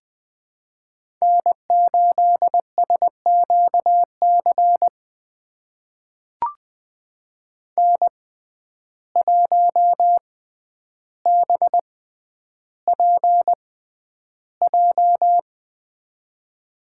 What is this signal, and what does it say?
Morse code, heard as a keyed tone, about 20 wpm, spaced as for 10 wpm, 700 Hz. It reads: N8SQC N1BPJ
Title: USA Call Signs - 5 Characters - 6x Character Spacing - 20wpm